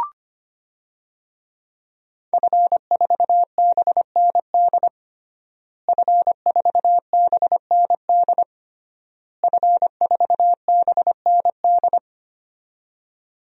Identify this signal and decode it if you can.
Morse code, heard as a keyed tone, about 25 wpm, 700 Hz. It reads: F4BND F4BND F4BND
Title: DXCC-100 Call Signs - 5 Characters - Repeated 3x - 25wpm